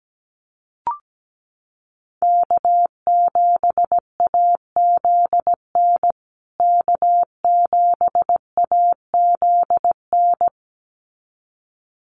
Morse code, heard as a keyed tone, about 17 wpm, 700 Hz. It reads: K7AZN K7AZN
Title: USA Call Signs Repeated 2x 17wpm